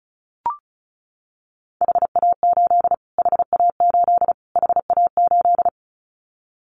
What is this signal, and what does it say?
Morse code, heard as a keyed tone, about 34 wpm, 700 Hz. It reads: HA8 HA8 HA8
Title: Callsign Prefixes for Top 25 DXCC - Repeated 3x - 35wpm